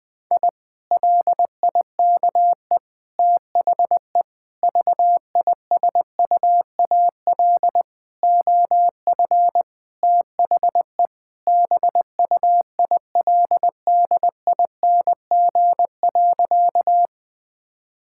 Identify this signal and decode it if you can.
Morse code, heard as a keyed tone, about 20 wpm, 700 Hz. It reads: I LIKE THE VISUAL OF THE BUILDING.
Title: Sentences from Top 1000 Words - Review Part 6 - 20wpm